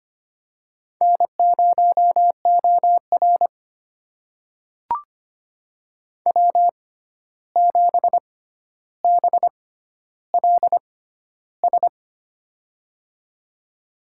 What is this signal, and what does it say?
Morse code, heard as a keyed tone, about 25 wpm, spaced as for 12 wpm, 700 Hz. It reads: N0OR W7BLS
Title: USA Call Signs - 6x Character Spacing - 25wpm